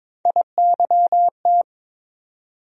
Morse code, heard as a keyed tone, about 22 wpm, 700 Hz. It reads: IYT
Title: USA Call Sign Postfixes - 3 Letters 22wpm